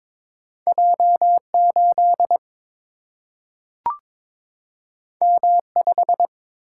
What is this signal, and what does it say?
Morse code, heard as a keyed tone, about 22 wpm, 700 Hz. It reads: J8 M5
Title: Single-Letter Single-Number 22wpm